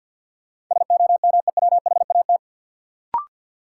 Morse code, heard as a keyed tone, about 53 wpm, 700 Hz. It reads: SOMEWHAT
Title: Sentences from Top 1000 Words - Review Part 1 - New Word Drill - 50wpm